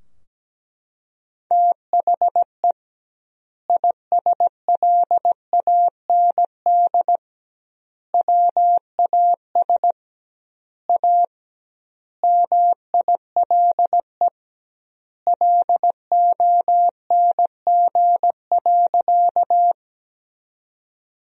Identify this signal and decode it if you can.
Morse code, heard as a keyed tone, about 17 wpm, 700 Hz. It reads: THE ISLAND WAS A MILE LONG.
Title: 2x Word-Spacing - Sentences from Top 500 Words - 17wpm